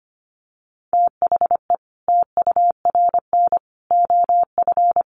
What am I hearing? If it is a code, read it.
Morse code, heard as a keyed tone, about 25 wpm, 700 Hz. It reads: THE TURN OF
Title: Sets of 5 Words 25wpm